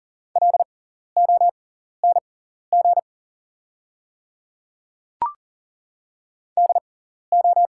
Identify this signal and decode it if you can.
Morse code, heard as a keyed tone, about 38 wpm, spaced as for 14 wpm, 690 Hz. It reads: LONG DO
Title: 6x Character Spacing - Top 100 Words - 40wpm